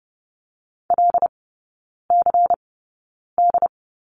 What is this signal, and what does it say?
Morse code, heard as a keyed tone, about 30 wpm, spaced as for 9 wpm, 690 Hz. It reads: LCD
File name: Lesson 16e - Letter c - Three Characters 30wpm